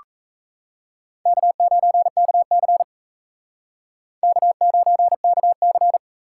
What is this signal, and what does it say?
Morse code, heard as a keyed tone, about 33 wpm, 700 Hz. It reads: K9KC K9KC
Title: Speed Racing - USA 1x2 Call Signs 28wpm